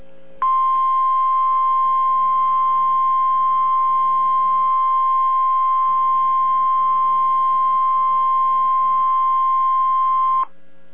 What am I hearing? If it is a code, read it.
Emergency Alert System (SAME) tones (the 1050 Hz attention tone). It is heard from 0.4 to 10.5 s.